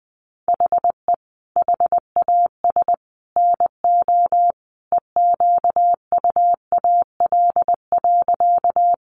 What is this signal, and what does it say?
Morse code, heard as a keyed tone, about 20 wpm, 700 Hz. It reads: HE HAS NO EQUAL.
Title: Sentences from Top 1400 Words 20wpm